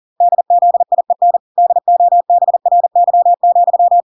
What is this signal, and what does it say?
Morse code, heard as a keyed tone, about 40 wpm, 700 Hz. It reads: DZIEN DOBRY,